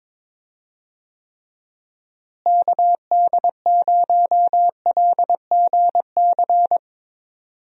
Morse code, heard as a keyed tone, about 21 wpm, 700 Hz. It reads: KD0LGC
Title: USA Call Signs - 6 Characters - Repeated 2x - 22wpm